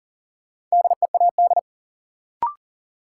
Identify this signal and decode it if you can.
Morse code, heard as a keyed tone, about 37 wpm, 700 Hz. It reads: DEAD
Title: Sentences from Top 1500 Words - New Word Drill Alt - 40wpm